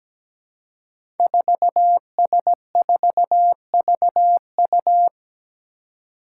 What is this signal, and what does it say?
Morse code, heard as a keyed tone, about 17 wpm, 700 Hz. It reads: 4S4VU
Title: Five Challenging Letter-Number Combinations 17wpm